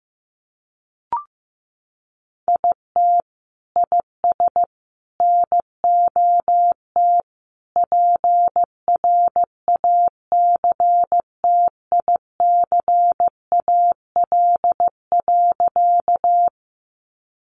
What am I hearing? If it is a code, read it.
Morse code, heard as a keyed tone, about 15 wpm, 700 Hz. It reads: IT IS NOT PRACTICAL.
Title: Sentences from Top 1400 Words 15wpm